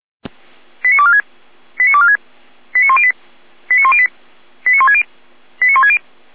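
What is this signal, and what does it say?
Five-tone selective call (ZVEI), heard as quick runs of five tones, tones of about 70 ms. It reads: 89246 89246 89108 89108 89160 89160
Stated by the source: ZVEI five-tone selective call 89560, received on SDR.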